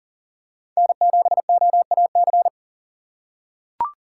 Morse code, heard as a keyed tone, about 40 wpm, 700 Hz. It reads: N7OAC
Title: USA Call Signs - 5 Characters - 2x Character Spacing - 40wpm